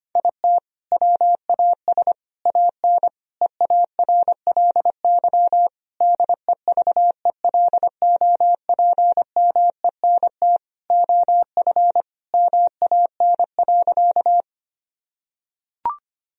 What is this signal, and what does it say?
Morse code, heard as a keyed tone, about 25 wpm, 700 Hz. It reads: IT WAS AN EARLY DEVELOPMENT OF MAN.
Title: Sentences from Top 1000 Words - Review Part 4 - 25wpm